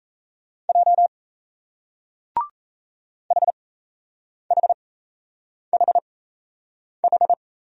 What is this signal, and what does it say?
Morse code, heard as a keyed tone, about 38 wpm, spaced as for 8 wpm, 700 Hz. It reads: J HHHH